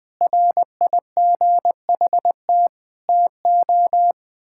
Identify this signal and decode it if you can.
Morse code, heard as a keyed tone, about 20 wpm, 700 Hz. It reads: RIGHT TO
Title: Sentences from Top 1000 Words - Review Part 6 - 20wpm